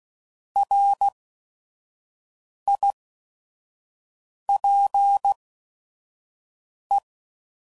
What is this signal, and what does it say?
Morse code, heard as a keyed tone, about 16 wpm, spaced as for 5 wpm, 800 Hz. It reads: RIPE
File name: Prefixes / By First Letter / Rs